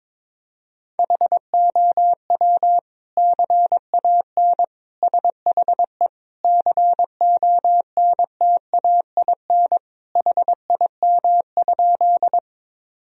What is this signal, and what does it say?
Morse code, heard as a keyed tone, about 21 wpm, 700 Hz. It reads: HOW CAN SHE CONTAIN HIM?
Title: Sentences from Top 1000 Words - Review Part 2 - 22wpm